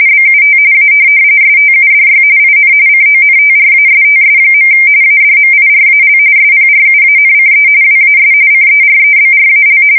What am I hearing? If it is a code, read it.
RTTY, about 45.45 baud. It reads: BUT A STUDY IN CONTRASTS. ADMIRAL SPRUANCE WAS CEREBRAL
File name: fsk170 / USS-New-Jersey